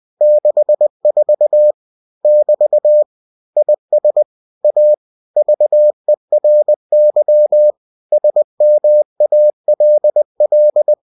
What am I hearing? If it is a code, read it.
Morse code, heard as a keyed tone, about 20 wpm, 600 Hz. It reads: 64 <BT> IS A VERY SMALL
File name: cw-books / winnie-the-pooh / winnie-20wpm-page-0064